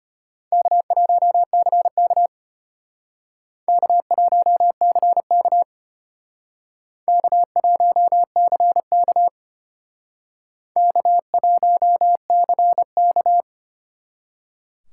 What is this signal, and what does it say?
Morse code, heard as a keyed tone, about 32 wpm, 700 Hz. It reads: K1CK K1CK K1CK K1CK